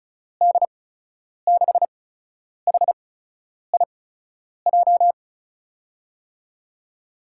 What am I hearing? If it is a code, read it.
Morse code, heard as a keyed tone, about 33 wpm, spaced as for 9 wpm, 700 Hz. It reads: D6HIJ
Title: USA Call Signs - 8x Character Spacing - 35wpm